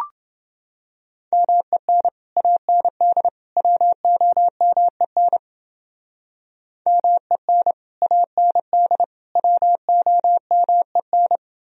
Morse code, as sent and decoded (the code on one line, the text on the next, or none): -- . -. / .- -. -.. / .-- --- -- . -. / -- . -. / .- -. -.. / .-- --- -- . -.
MEN AND WOMEN MEN AND WOMEN